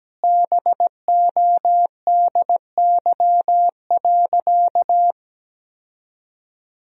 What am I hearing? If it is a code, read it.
Morse code, heard as a keyed tone, about 17 wpm, 700 Hz. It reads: BODY.
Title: 2x Word-Spacing - Sentences from Top 400 Words - 17wpm